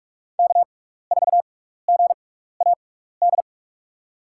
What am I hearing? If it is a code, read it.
Morse code, heard as a keyed tone, about 45 wpm, spaced as for 15 wpm, 700 Hz. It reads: K4GAD
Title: USA Call Signs - 6x Character Spacing - 45wpm